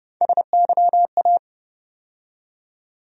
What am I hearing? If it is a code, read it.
Morse code, heard as a keyed tone, about 30 wpm, 700 Hz. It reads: SYA
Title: USA Call Sign Postfixes - 3 Letters 30wpm